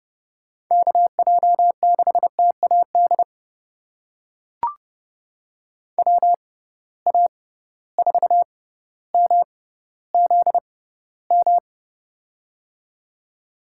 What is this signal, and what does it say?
Morse code, heard as a keyed tone, about 30 wpm, spaced as for 15 wpm, 700 Hz. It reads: KJ6TAD WA4MZM
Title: USA Call Signs - 6 Characters - 6x Character Spacing - 30wpm